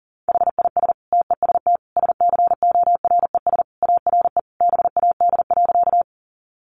Morse code, heard as a keyed tone, about 40 wpm, 700 Hz. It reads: HIS TEST SCORES ARE BAD.
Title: Additional Easy Phrases 40wpm